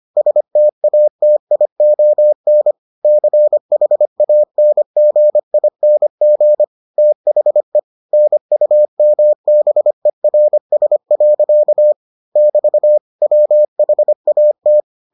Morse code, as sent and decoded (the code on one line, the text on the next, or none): ... - .- - .. --- -. / -.-. .... .- -. --. .. -. --. / - .... / -. ..- -- -... .-. ... .-.-.- / -...- / .-- .... .- -
STATION CHANGING THE NUMBERS. <BT> WHAT